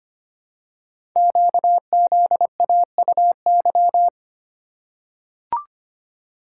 Morse code, heard as a keyed tone, about 25 wpm, 700 Hz. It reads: QZAUY
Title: Code Groups - 5 Letters - 25wpm